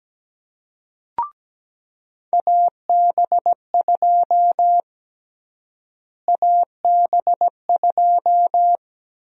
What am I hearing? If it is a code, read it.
Morse code, heard as a keyed tone, about 17 wpm, 700 Hz. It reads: AB2 AB2
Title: USA Call Sign Prefixes - Repeated 2x - 17wpm